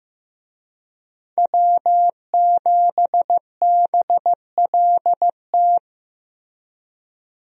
Morse code, heard as a keyed tone, about 14 wpm, 700 Hz. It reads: W7BLT
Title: USA Call Signs - 5 Characters - Repeated 2x - 15wpm